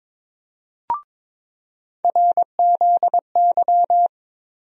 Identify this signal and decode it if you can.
Morse code, heard as a keyed tone, about 22 wpm, 700 Hz. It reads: RZY
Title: USA Call Sign Postfixes - 3 Letters 22wpm